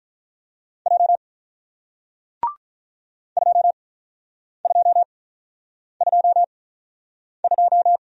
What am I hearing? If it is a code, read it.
Morse code, heard as a keyed tone, about 48 wpm, spaced as for 9 wpm, 700 Hz. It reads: J 2222